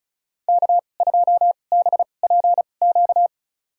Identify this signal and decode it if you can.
Morse code, heard as a keyed tone, about 36 wpm, spaced as for 25 wpm, 700 Hz. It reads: K2BPQ
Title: USA Call Signs - 5 Characters - 2x Character Spacing - 35wpm